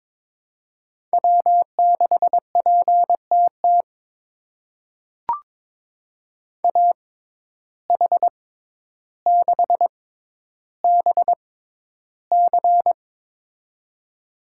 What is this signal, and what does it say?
Morse code, heard as a keyed tone, about 22 wpm, spaced as for 11 wpm, 700 Hz. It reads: W6PTT AH6BC
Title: USA Call Signs - 5 Characters - 6x Character Spacing - 22wpm